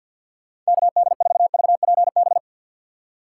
Morse code, heard as a keyed tone, about 62 wpm, 700 Hz. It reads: KD4VPB
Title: Speed Racing - USA 2x3 Call Signs 50wpm